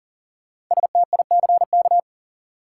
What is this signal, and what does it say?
Morse code, heard as a keyed tone, about 40 wpm, 700 Hz. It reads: STICK
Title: Sentences from Top 1200 Words - New Word Drill Alt - 40wpm